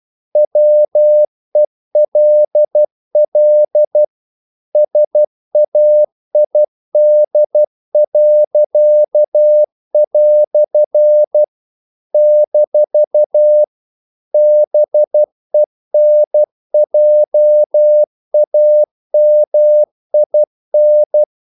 Morse code, heard as a keyed tone, about 12 wpm, 600 Hz. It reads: WELL SAID." - BENJAMIN
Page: English quotations in CW (telegraphy - morse)